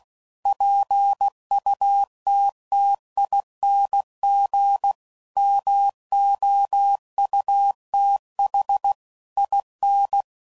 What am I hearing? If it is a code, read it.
Morse code, heard as a keyed tone, about 16 wpm, 800 Hz. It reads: PUTTING MOUTH IN